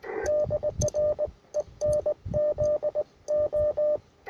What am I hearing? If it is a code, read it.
Morse code, heard as a keyed tone, about 19 wpm, 600 Hz. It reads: DR ENZO
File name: attachments / UR4MCK IZ5GVP